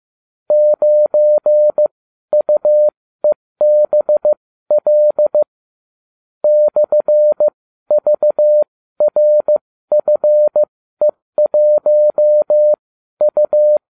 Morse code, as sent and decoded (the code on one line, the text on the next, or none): ----. ..- . -... .-.. / -..-. ...- .-. ..-. .---- ..-
9UEBL /VRFE1U